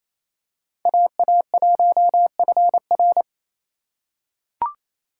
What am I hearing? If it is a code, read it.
Morse code, heard as a keyed tone, about 28 wpm, 700 Hz. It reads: AA1FR
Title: USA Call Signs - 5 Characters - 6x Character Spacing - 28wpm